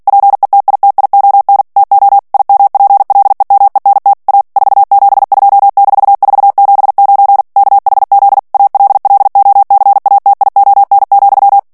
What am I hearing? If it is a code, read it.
Morse code, heard as a keyed tone, about 48 wpm, 800 Hz. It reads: PETITION TO IMPLEMENT A 472-479 KHZ ALLOCATION,
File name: ANGLAIS texte clair 3 vitesse 45 mots minute